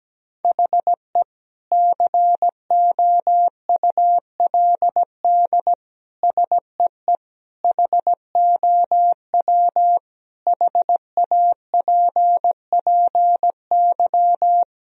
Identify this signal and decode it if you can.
Morse code, heard as a keyed tone, about 17 wpm, 700 Hz. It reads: HE COULD SEE HOW HAPPY
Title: Sentences from Top 1000 Words - Review Part 3 - 17wpm